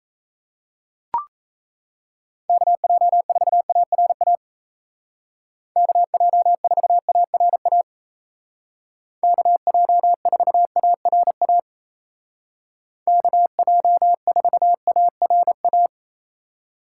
Morse code, heard as a keyed tone, about 37 wpm, 700 Hz. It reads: KJ4ARA KJ4ARA KJ4ARA KJ4ARA